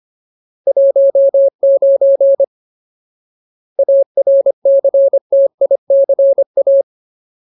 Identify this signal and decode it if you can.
Morse code, heard as a keyed tone, about 25 wpm, 550 Hz. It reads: 19 ARCTICA
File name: Ultimate Word List / 25WPM Ultimate